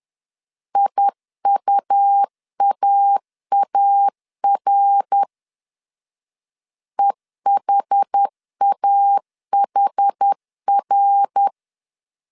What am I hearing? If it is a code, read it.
Morse code, heard as a keyed tone, about 10 wpm, 800 Hz. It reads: IUAAR EHAHR
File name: CW / DET / 1 3 1500C S B 10wpm 3335